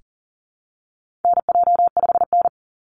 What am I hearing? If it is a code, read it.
Morse code, heard as a keyed tone, about 40 wpm, 700 Hz. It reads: NJ5N
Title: Call Signs Repeated 3x - 40wpm